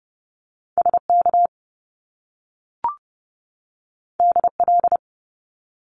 Morse code, heard as a keyed tone, about 30 wpm, 700 Hz. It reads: SK DL
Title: Top Two Letter Combinations in Words 30wpm